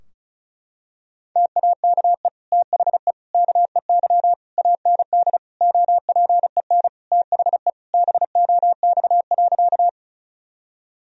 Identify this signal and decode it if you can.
Morse code, heard as a keyed tone, about 36 wpm, 700 Hz. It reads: TAKE THE KEY AND OPEN THE BOX.
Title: Sentences from Top 1000 Words - Review Part 4 - 35wpm